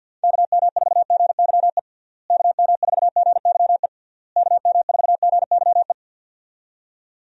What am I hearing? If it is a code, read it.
Morse code, heard as a keyed tone, about 52 wpm, 700 Hz. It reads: KM4GYE KM4GYE KM4GYE
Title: USA Call Signs - 6 Characters - Repeated 3x - 50wpm